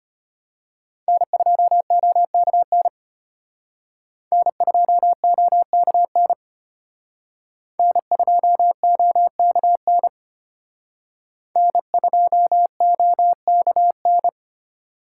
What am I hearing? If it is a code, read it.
Morse code, heard as a keyed tone, about 32 wpm, 700 Hz. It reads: N2OKN N2OKN N2OKN N2OKN